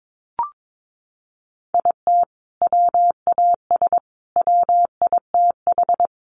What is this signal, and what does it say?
Morse code, heard as a keyed tone, about 22 wpm, 700 Hz. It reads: IT WAS WITH
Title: Sentences from Top 1500 Words 22wpm